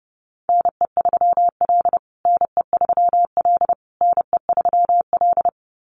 Morse code, heard as a keyed tone, about 30 wpm, 700 Hz. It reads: NE3L NE3L NE3L
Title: USA Call Signs Repeated 3x 30wpm